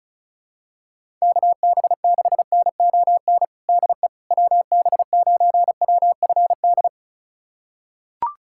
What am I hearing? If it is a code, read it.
Morse code, heard as a keyed tone, about 36 wpm, 700 Hz. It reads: KB6NON DE WB9WFD
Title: USA Callsign DE Callsign 35wpm